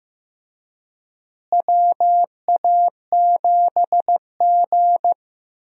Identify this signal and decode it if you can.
Morse code, heard as a keyed tone, about 15 wpm, 700 Hz. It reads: WA7G
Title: USA Call Signs - 4 Characters - Repeated 2x - 15wpm